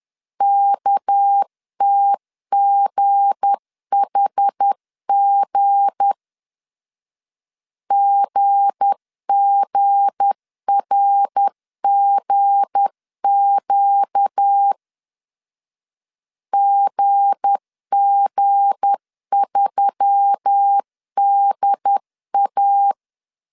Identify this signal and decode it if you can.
Morse code, heard as a keyed tone, about 10 wpm, 800 Hz. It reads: KTGHG GGRGQ GG3DA
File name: LEZIONI / 7 GF9Z / 7 G 10wpm 2501